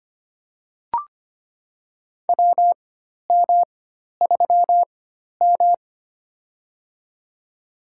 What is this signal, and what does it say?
Morse code, heard as a keyed tone, about 25 wpm, spaced as for 12 wpm, 690 Hz. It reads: WM3M
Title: USA Call Signs - 4 Characters - 4x Character Spacing - 25wpm